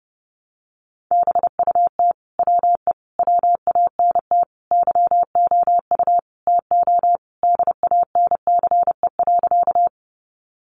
Morse code, heard as a keyed tone, about 29 wpm, 700 Hz. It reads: BUT WE WANT YOU TO DANCE.
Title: Sentences from Top 700 Words 30wpm